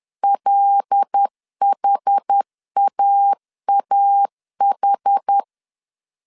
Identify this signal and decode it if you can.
Morse code, heard as a keyed tone, about 10 wpm, 800 Hz. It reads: LHAAH
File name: CW / DET / 1 9 1500C S B 10wpm 4439